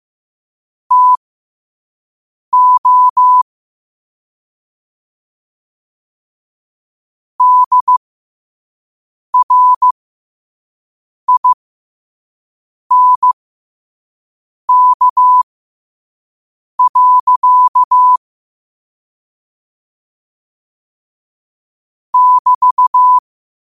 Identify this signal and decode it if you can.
Morse code, heard as a keyed tone, about 15 wpm, spaced as for 5 wpm, 1000 Hz. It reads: TO DRINK. <BT>
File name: qotdfiles / 5wpm